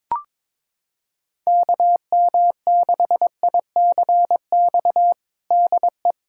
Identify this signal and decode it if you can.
Morse code, heard as a keyed tone, about 22 wpm, 700 Hz. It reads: KM6ICX DE
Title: USA Callsign DE Callsign - 6 Characters - 22wpm